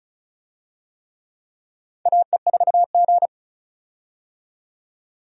Morse code, heard as a keyed tone, about 34 wpm, 700 Hz. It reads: AE4G
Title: USA Call Signs - 4 Characters - 35wpm